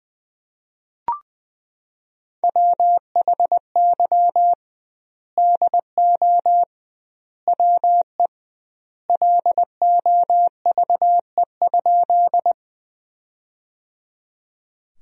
Morse code, heard as a keyed tone, about 20 wpm, 700 Hz. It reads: WHY DO WE LOVE?
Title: 2x Word-Spacing - Sentences from Top 400 Words - 20wpm